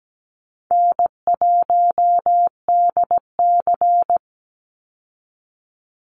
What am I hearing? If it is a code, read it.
Morse code, heard as a keyed tone, about 17 wpm, 700 Hz. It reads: N1DC
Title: Easy Call Signs - 17wpm